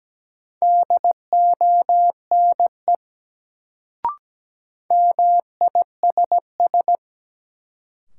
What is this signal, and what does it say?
Morse code, heard as a keyed tone, about 17 wpm, 700 Hz. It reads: DONE MISS
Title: Sentences from Top 500 Words - New Word Drill - 17wpm